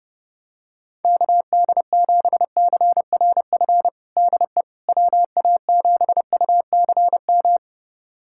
Morse code, heard as a keyed tone, about 32 wpm, 700 Hz. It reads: KD7CRF DE WA7UCM
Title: USA Callsign DE Callsign - 6 Characters - 30wpm